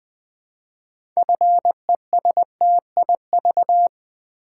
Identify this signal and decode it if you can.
Morse code, heard as a keyed tone, about 20 wpm, 700 Hz. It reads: FESTIV
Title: Build A Word - 2 Letters At A Time - 8-Letter Words - 20wpm